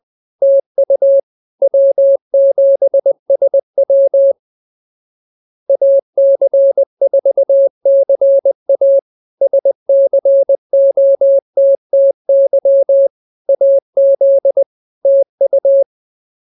Morse code, heard as a keyed tone, about 20 wpm, 550 Hz. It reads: TU W7SW AC4CA SCOTTY AZ TU
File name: CWT-207-20